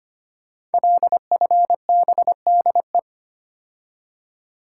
Morse code, heard as a keyed tone, about 25 wpm, 700 Hz. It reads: LFBDE
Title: Code Groups - 5 Letters - 25wpm